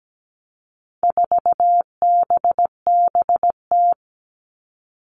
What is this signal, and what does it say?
Morse code, heard as a keyed tone, about 17 wpm, 700 Hz. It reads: 4BBT
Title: All Cut Numbers - 4-Digit 17wpm